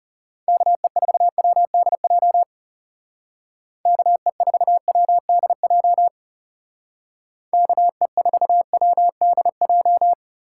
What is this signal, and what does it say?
Morse code, heard as a keyed tone, about 37 wpm, 700 Hz. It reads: KE4WDJ KE4WDJ KE4WDJ
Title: Speed Racing - USA Call Signs 30wpm